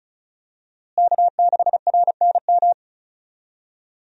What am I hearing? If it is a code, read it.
Morse code, heard as a keyed tone, about 36 wpm, 700 Hz. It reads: K6RNM
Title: USA Call Signs - 5 Characters - Repeated 3x - 35wpm